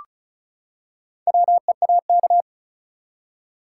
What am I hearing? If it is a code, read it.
Morse code, heard as a keyed tone, about 34 wpm, 700 Hz. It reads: WEAK
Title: Four Letter Words - 35wpm